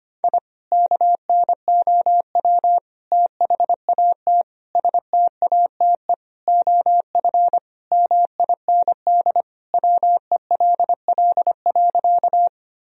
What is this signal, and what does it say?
Morse code, heard as a keyed tone, about 25 wpm, 700 Hz. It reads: I KNOW THAT STATE OF MIND WELL.
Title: Sentences from Top 1000 Words - Review Part 2 - 25wpm